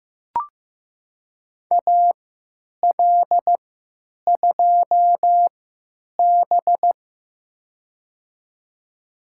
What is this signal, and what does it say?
Morse code, heard as a keyed tone, about 15 wpm, spaced as for 9 wpm, 700 Hz. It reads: AL2B
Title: USA Call Signs - 4 Characters - 3x Character Spacing - 15wpm